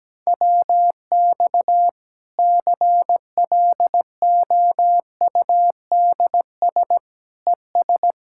Morse code, heard as a keyed tone, about 17 wpm, 700 Hz. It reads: WX CLOUDS ES
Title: Practice QSO - Exchange 2 - Info - 17wpm